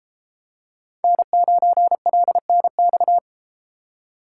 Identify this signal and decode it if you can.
Morse code, heard as a keyed tone, about 32 wpm, 700 Hz. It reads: N9LNX